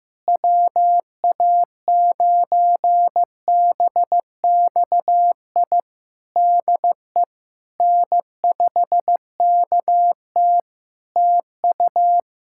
Morse code, heard as a keyed tone, about 15 wpm, 700 Hz. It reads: WA9BXI DE N5KT TU
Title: Practice QSO - Exchange 4 - Ending - 15wpm